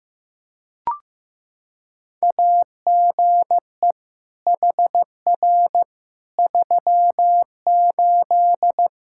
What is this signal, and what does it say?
Morse code, heard as a keyed tone, about 15 wpm, 700 Hz. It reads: AGE HR 38
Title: K1USN SST QSO Practice 15wpm